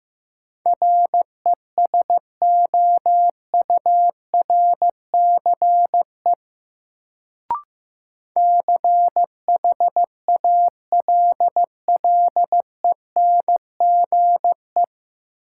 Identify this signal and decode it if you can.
Morse code, heard as a keyed tone, about 15 wpm, 700 Hz. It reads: RESOURCE CHALLENGE
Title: Sentences from Top 1300 Words - New Word Drill Alt - 15wpm